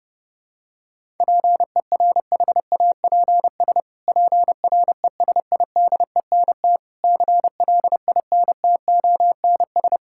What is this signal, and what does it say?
Morse code, heard as a keyed tone, about 32 wpm, 700 Hz. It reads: PERHAPS PRESIDENT CLINTONS
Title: Pangram Challenge v2 - 30wpm